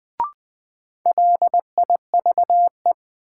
Morse code, heard as a keyed tone, about 20 wpm, 700 Hz. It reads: LIVE
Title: Sentences from Top 200 Words - New Word Drill - 20wpm